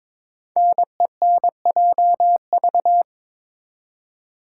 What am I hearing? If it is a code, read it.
Morse code, heard as a keyed tone, about 22 wpm, 700 Hz. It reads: NENJV
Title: Code Groups - 5 Letters - 22wpm